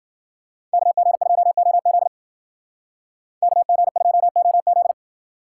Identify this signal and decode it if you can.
Morse code, heard as a keyed tone, about 48 wpm, 700 Hz. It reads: KG2OZ KG2OZ
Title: Speed Racing - USA 2x2 Call Signs 40wpm